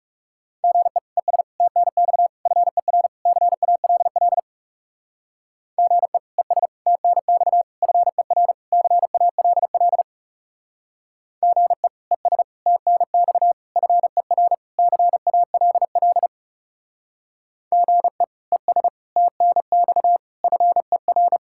Morse code, heard as a keyed tone, about 37 wpm, 700 Hz. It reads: GE ES TNX FER CALL GE ES TNX FER CALL GE ES TNX FER CALL GE ES TNX FER
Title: Speed-Racing Common QSO Phrases - 30wpm